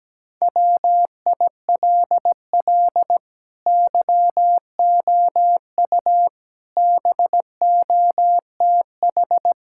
Morse code, heard as a keyed tone, about 17 wpm, 700 Hz. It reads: WILL YOU BOTH
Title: Sentences from Top 1600 Words 17wpm